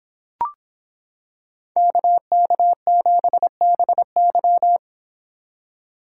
Morse code, heard as a keyed tone, about 26 wpm, 700 Hz. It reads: KK7BY